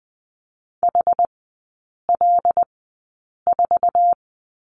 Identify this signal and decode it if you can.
Morse code, heard as a keyed tone, about 20 wpm, spaced as for 8 wpm, 700 Hz. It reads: HL4
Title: 074 Lesson 13e - Letter L - Three Characters 20wpm